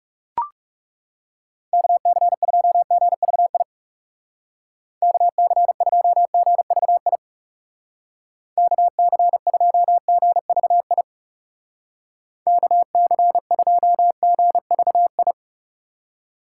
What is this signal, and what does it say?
Morse code, heard as a keyed tone, about 37 wpm, 700 Hz. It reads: KC2GVI KC2GVI KC2GVI KC2GVI
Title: Speed Racing - USA 2x3 Call Signs 30wpm